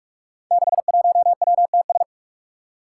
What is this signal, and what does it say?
Morse code, heard as a keyed tone, about 44 wpm, 700 Hz. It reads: 61WTS
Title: Five Challenging Letter-Number Combinations 45wpm